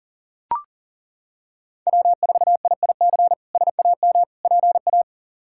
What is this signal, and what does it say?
Morse code, heard as a keyed tone, about 40 wpm, 700 Hz. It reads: W4IIC SAM PA
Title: K1USN SST QSO Practice - 40wpm